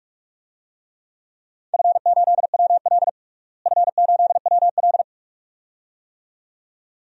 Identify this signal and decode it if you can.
Morse code, heard as a keyed tone, about 46 wpm, 700 Hz. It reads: F8WL F8WL
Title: DXCC-100 Call Signs - 4 Characters - Repeated 2x - 45wpm